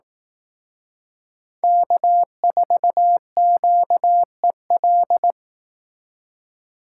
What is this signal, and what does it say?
Morse code, heard as a keyed tone, about 18 wpm, 700 Hz. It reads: K4QEL
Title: Speed Racing - USA Call Signs 15wpm